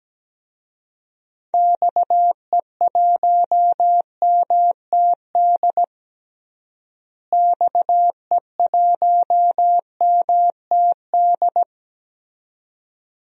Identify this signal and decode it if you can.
Morse code, heard as a keyed tone, about 17 wpm, 700 Hz. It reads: XE1MTD XE1MTD
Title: DXCC-100 Call Signs - 6 Characters - Repeated 2x - 17wpm